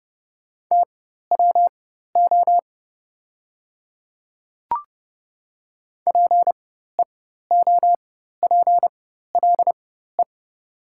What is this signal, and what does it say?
Morse code, heard as a keyed tone, about 30 wpm, spaced as for 14 wpm, 690 Hz. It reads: TWO PEOPLE
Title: 4x Character Spacing - Top 100 Words - 30wpm